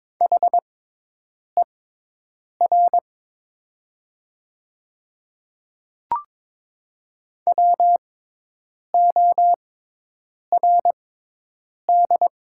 Morse code, heard as a keyed tone, about 22 wpm, spaced as for 8 wpm, 700 Hz. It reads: HER WORD
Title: 6x Character Spacing - Top 100 Words - 22wpm